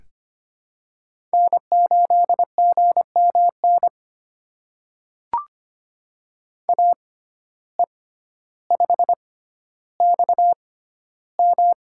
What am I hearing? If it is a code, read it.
Morse code, heard as a keyed tone, about 25 wpm, spaced as for 13 wpm, 700 Hz. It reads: N8GMN AE5XM
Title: USA Call Signs - 5 Characters - 6x Character Spacing - 25wpm